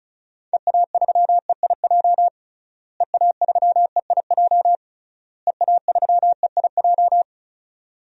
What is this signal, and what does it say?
Morse code, heard as a keyed tone, about 36 wpm, 700 Hz. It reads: EA3EIJ EA3EIJ EA3EIJ
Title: DXCC-100 Call Signs - 6 Characters - Repeated 3x - 35wpm